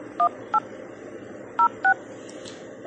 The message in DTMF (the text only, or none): 18 06